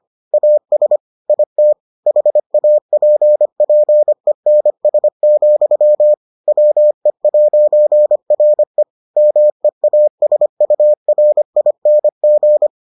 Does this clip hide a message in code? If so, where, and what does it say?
Yes: Morse code from 0.3 to 12.7 s, heard as a keyed tone, about 25 wpm, 600 Hz. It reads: AS IT HAPPENS, WE'RE MEASURING